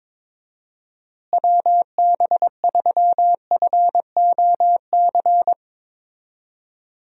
Morse code, heard as a keyed tone, about 22 wpm, 700 Hz. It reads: WB3FOC
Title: USA Call Signs - 6 Characters - Repeated 2x - 22wpm